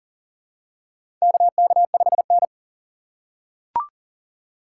Morse code, heard as a keyed tone, about 40 wpm, 700 Hz. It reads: KK5N
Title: USA Call Signs - 4x Character Spacing - 40wpm